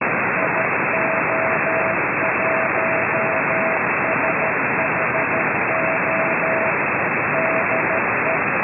Morse code, heard as a keyed tone, about 13 wpm, 650 Hz, with noise. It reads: IO1SHO DE